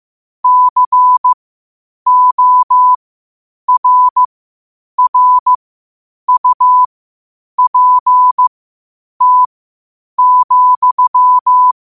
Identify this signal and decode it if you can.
Morse code, heard as a keyed tone, about 15 wpm, spaced as for 8 wpm, 1000 Hz. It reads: CORRUPT,
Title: 8 WPM morse code quotes for Thu, 29 Feb 2024